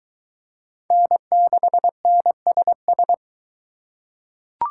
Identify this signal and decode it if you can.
Morse code, heard as a keyed tone, about 22 wpm, 700 Hz. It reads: N6NSS